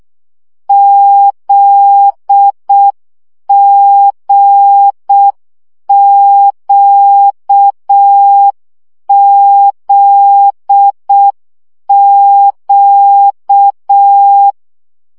Morse code, heard as a keyed tone, about 6 wpm, 800 Hz. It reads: ZGQZQ
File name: CW gyakorlas g z q 5 baud